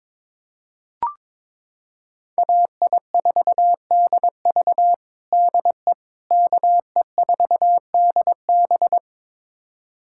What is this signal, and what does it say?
Morse code, heard as a keyed tone, about 22 wpm, 700 Hz. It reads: AI4DV DE KE4DB
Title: USA Callsign DE Callsign - 5 Characters - 22wpm